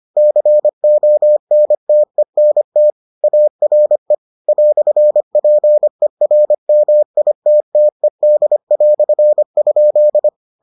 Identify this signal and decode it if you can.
Morse code, heard as a keyed tone, about 25 wpm, 600 Hz. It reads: CONTENT ARE "PERMITTED"?